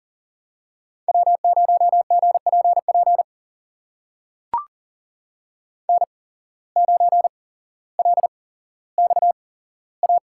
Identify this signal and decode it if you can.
Morse code, heard as a keyed tone, about 40 wpm, spaced as for 17 wpm, 700 Hz. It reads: W0GPP N9LXA
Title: USA Call Signs - 5 Characters - 8x Character Spacing - 40wpm